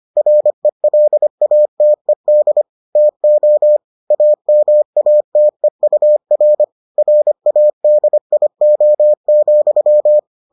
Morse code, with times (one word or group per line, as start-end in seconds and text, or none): 0.2-2.6 RELATED
3.0-3.8 TO
4.1-6.6 AMATEUR
7.0-10.2 RADIO,